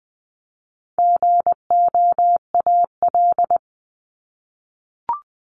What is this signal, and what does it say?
Morse code, heard as a keyed tone, about 20 wpm, 700 Hz. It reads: GOAL
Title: Top 751-1000 Words 20wpm